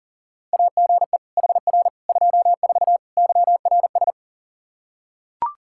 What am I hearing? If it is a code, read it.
Morse code, heard as a keyed tone, about 37 wpm, 700 Hz. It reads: AGE HR 24 YRS
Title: Practice QSO - Exchange 3 - More Info - 40wpm